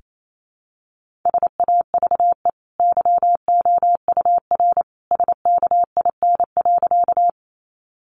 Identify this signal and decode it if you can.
Morse code, heard as a keyed tone, about 28 wpm, 700 Hz. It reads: SAVE YOUR SKIN.
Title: Sentences from Top 1300 Words 28wpm